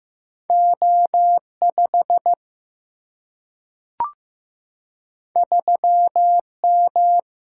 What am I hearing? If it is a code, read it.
Morse code, heard as a keyed tone, about 15 wpm, 700 Hz. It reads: O5 3M
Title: Single-Letter Single-Number 15wpm